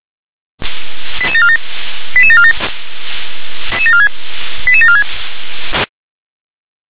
Five-tone selective call (ZVEI), heard as quick runs of five tones, tones of about 70 ms. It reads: 88747 88747 88746
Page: SDR/ZVEI